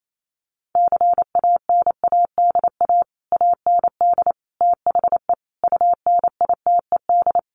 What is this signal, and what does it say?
Morse code, heard as a keyed tone, about 28 wpm, 700 Hz. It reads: CANADA AND THE UNITED
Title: Sets of 5 Words Encore 28wpm